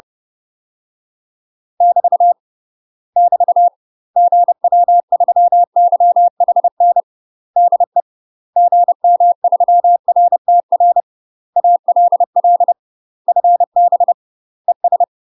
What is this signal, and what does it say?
Morse code, heard as a keyed tone, about 30 wpm, 700 Hz. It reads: <BT> <BT> GW3YHN DE GM3RTR ALL FB ES
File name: UK MASTER QSOs TRAINING / 30WPM / UK MASTER QSOs 0009